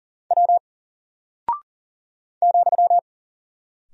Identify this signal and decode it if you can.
Morse code, heard as a keyed tone, about 40 wpm, 700 Hz. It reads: W ,
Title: Single Character (Letter:Number:Punctuation:Prosign) Alt 40wpm